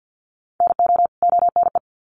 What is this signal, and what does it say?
Morse code, heard as a keyed tone, about 51 wpm, 700 Hz. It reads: NO ONE
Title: Sets of 2 Words 50wpm